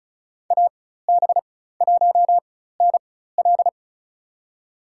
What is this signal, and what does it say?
Morse code, heard as a keyed tone, about 34 wpm, spaced as for 16 wpm, 700 Hz. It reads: AB1NL
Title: USA Call Signs - 4x Character Spacing - 35wpm